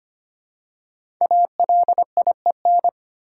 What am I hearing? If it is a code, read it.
Morse code, heard as a keyed tone, about 25 wpm, 700 Hz. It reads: ALIEN
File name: 5 Letter Words - First 3 Letters Given - 25wpm